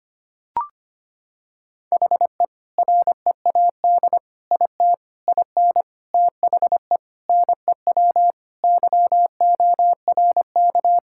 Morse code, heard as a keyed tone, about 25 wpm, 700 Hz. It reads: HE READ IT IN THE NEW YORK
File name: practice / 25wpm / Sentences with 4-Word Phrases - 25wpm